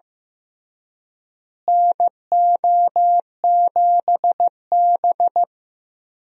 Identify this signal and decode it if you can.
Morse code, heard as a keyed tone, about 15 wpm, 700 Hz. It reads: NO7B
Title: USA Call Signs - 4 Characters - Repeated 2x - 15wpm